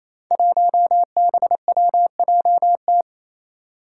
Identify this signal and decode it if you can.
Morse code, heard as a keyed tone, about 29 wpm, 700 Hz. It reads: 1BWJT
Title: Five Challenging Letter-Number Combinations 28wpm